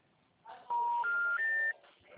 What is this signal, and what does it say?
Special information tone.